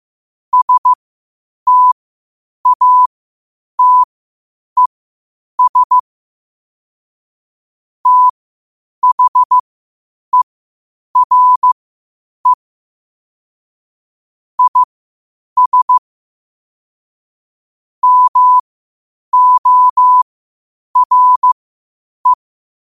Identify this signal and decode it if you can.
Morse code, heard as a keyed tone, about 15 wpm, spaced as for 8 wpm, 1000 Hz. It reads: STATES THERE IS MORE
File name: qotdfiles / 8wpm